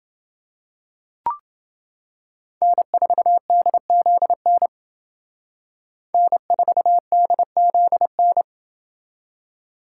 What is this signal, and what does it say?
Morse code, heard as a keyed tone, about 28 wpm, 700 Hz. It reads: N4DZN N4DZN